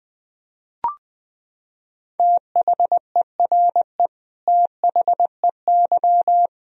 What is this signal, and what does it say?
Morse code, heard as a keyed tone, about 20 wpm, 700 Hz. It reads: THERE THEY
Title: Sentences from Top 100 Words and Abbreviations 20wpm